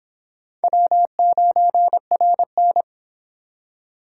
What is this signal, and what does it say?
Morse code, heard as a keyed tone, about 26 wpm, 700 Hz. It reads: W9RN